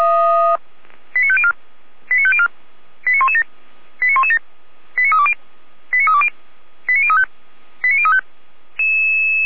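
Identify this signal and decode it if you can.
Five-tone selective call (ZVEI), heard as quick runs of five tones, tones of about 70 ms. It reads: 89504 89504 89117 89117 89320 89320 89035 89035
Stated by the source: ZVEI five-tone selective call 89140 received on SDR, feuerwehr-orlamuende.org fire brigade server.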